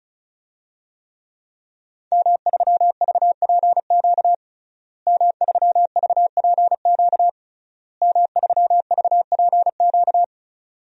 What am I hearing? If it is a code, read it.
Morse code, heard as a keyed tone, about 34 wpm, 700 Hz. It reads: M3VPQ M3VPQ M3VPQ
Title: DXCC-100 Call Signs - 5 Characters - Repeated 3x - 35wpm